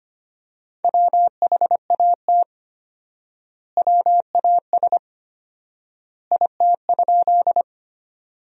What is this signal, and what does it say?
Morse code, heard as a keyed tone, about 25 wpm, 700 Hz. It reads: WHAT WAS IT?
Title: 4x Word-Spacing - Sentences from Top 100 Words - 25wpm